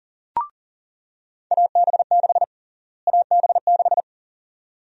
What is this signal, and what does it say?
Morse code, heard as a keyed tone, about 37 wpm, 700 Hz. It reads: AB6 AB6
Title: USA Call Sign Prefixes - Repeated 2x - 40wpm